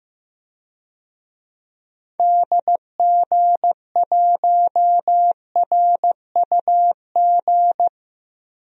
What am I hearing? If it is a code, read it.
Morse code, heard as a keyed tone, about 15 wpm, 700 Hz. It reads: DG1RUG
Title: DXCC-100 Call Signs - 6 Characters - 15wpm